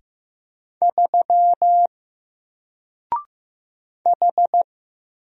Morse code, heard as a keyed tone, about 15 wpm, 700 Hz. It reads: 3 H
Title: Single Letter-Number Alt 15wpm